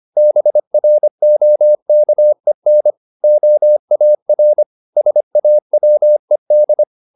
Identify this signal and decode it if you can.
Morse code, heard as a keyed tone, about 25 wpm, 600 Hz. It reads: BROKEN OAR SAWED